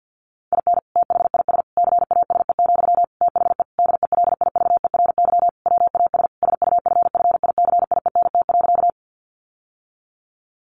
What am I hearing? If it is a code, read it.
Morse code, heard as a keyed tone, about 48 wpm, 700 Hz. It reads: IN THIS CASE, THE DELIVERY WAS SUFFICIENT.
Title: Sentences from Top 1500 Words 50wpm